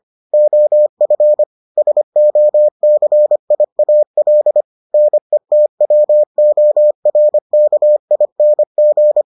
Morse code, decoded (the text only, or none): OF SOCIAL NETWORKING